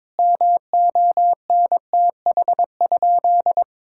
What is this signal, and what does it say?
Morse code, heard as a keyed tone, about 22 wpm, 700 Hz. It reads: MONTH?
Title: Sentences from Top 1000 Words - Review Part 5 - 22wpm